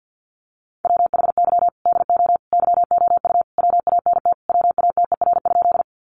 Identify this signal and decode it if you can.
Morse code, heard as a keyed tone, about 50 wpm, 700 Hz. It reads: WHY DO YOU WANT WATER?
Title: Sentences from Top 200 Words 50wpm